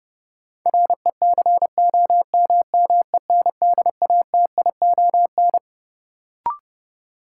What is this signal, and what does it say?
Morse code, heard as a keyed tone, about 31 wpm, 700 Hz. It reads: RECOMMENDATION
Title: Sentences from Top 1400 Words - New Word Drill - 30wpm